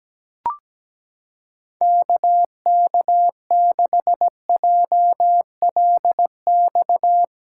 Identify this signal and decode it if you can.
Morse code, heard as a keyed tone, about 17 wpm, 700 Hz. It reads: KK6JLX
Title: USA Callsign DE Callsign 17wpm